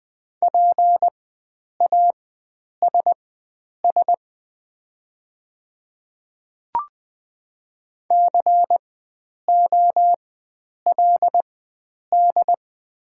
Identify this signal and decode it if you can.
Morse code, heard as a keyed tone, about 20 wpm, spaced as for 9 wpm, 700 Hz. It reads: PASS COLD
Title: 4x Character Spacing - Top 500 Words - 20wpm